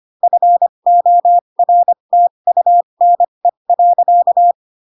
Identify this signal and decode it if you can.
Morse code, heard as a keyed tone, about 25 wpm, spaced as for 22 wpm, 700 Hz. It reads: FORTUNE.
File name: Pinocchio / Pinocchio ch3 18 25 ik1hge